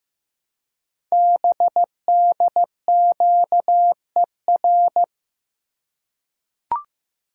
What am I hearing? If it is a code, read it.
Morse code, heard as a keyed tone, about 15 wpm, 700 Hz. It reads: BDQER
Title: Code Groups - 5 Letters - 15wpm